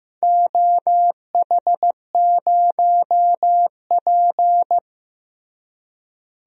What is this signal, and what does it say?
Morse code, heard as a keyed tone, about 15 wpm, 700 Hz. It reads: OH0P
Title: DXCC-100 Call Signs - 4 Characters - Repeated 2x - 15wpm